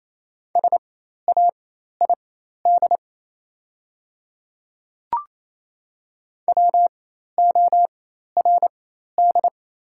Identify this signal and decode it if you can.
Morse code, heard as a keyed tone, about 27 wpm, spaced as for 13 wpm, 690 Hz. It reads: SAID WORD
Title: 4x Character Spacing - Top 200 Words - 28wpm